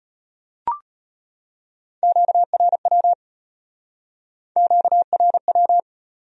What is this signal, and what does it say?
Morse code, heard as a keyed tone, about 33 wpm, 700 Hz. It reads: QRW QRW